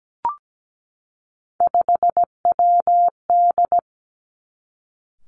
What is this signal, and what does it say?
Morse code, heard as a keyed tone, about 17 wpm, 700 Hz. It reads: 5WD
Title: Three Challenging Letter-Number Combinations 17wpm